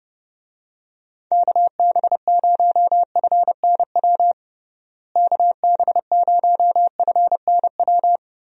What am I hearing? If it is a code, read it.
Morse code, heard as a keyed tone, about 30 wpm, 700 Hz. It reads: KB0FNW KB0FNW
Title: USA Call Signs - 6 Characters - Repeated 2x - 30wpm